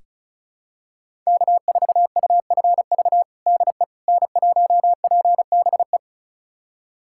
Morse code, heard as a keyed tone, about 36 wpm, 700 Hz. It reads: K4UFV DE N1PBE
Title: USA Callsign DE Callsign - 5 Characters - 35wpm